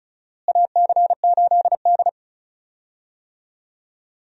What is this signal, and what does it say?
Morse code, heard as a keyed tone, about 34 wpm, 700 Hz. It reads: AC8D
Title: USA Call Signs - 4 Characters - Repeated 3x - 35wpm